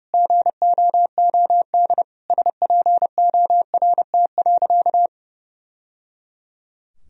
Morse code, heard as a keyed tone, about 29 wpm, 700 Hz. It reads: GOOD SPORT.
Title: Sentences from Top 1000 Words - Review Part 4 - 30wpm